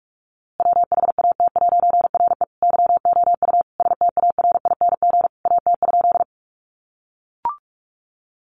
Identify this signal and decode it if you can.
Morse code, heard as a keyed tone, about 44 wpm, 700 Hz. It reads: WHAT'RE YOU STARING AT?
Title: A Sound Of Thunder 45wpm